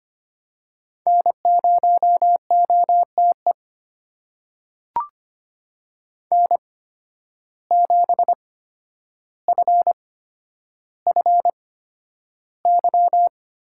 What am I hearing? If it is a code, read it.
Morse code, heard as a keyed tone, about 25 wpm, spaced as for 11 wpm, 700 Hz. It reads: N0OTE N7FFY
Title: USA Call Signs - 8x Character Spacing - 25wpm